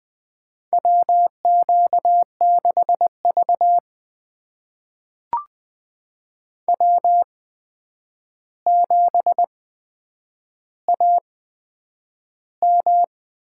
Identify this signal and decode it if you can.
Morse code, heard as a keyed tone, about 20 wpm, spaced as for 9 wpm, 700 Hz. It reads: WQ6V W7AM
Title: USA Call Signs - 4 Characters - 8x Character Spacing - 20wpm